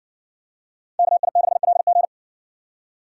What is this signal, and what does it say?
Morse code, heard as a keyed tone, about 60 wpm, 700 Hz. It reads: KI6LG